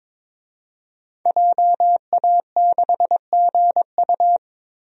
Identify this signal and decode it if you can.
Morse code, heard as a keyed tone, about 22 wpm, 700 Hz. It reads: JA6GU
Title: DXCC-100 Call Signs - 5 Characters - 22wpm